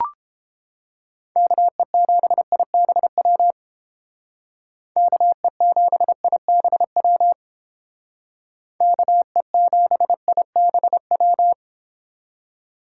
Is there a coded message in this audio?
Yes: Morse code from 1.4 to 11.5 s, heard as a keyed tone, about 30 wpm, 700 Hz. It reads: KE7IBW KE7IBW KE7IBW